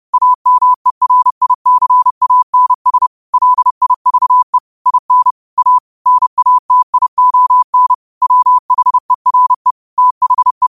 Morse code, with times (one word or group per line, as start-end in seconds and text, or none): 0.1-3.1 AMERICANS
3.3-4.6 LIVE
4.9-5.3 IN
5.6-5.8 A
6.1-7.9 NATION
8.2-9.7 WHERE
10.0-10.7 THE